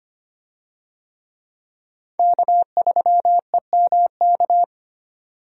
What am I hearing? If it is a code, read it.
Morse code, heard as a keyed tone, about 25 wpm, 700 Hz. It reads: K3EMK
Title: USA Call Signs - 5 Characters - 25wpm